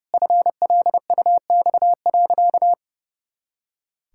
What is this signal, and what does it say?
Morse code, heard as a keyed tone, about 30 wpm, 700 Hz. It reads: FLUX.
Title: Pangram Challenge v2 - 30wpm